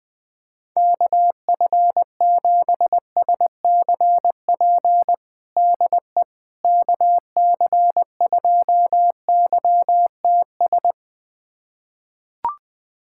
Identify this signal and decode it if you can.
Morse code, heard as a keyed tone, about 20 wpm, 700 Hz. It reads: KF7SCP DE KC2YTS
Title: USA Callsign DE Callsign - 6 Characters - 20wpm